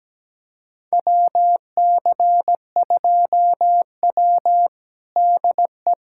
Morse code, heard as a keyed tone, about 17 wpm, 700 Hz. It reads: WC2W DE
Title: USA Callsign DE Callsign - 4 Characters - 17wpm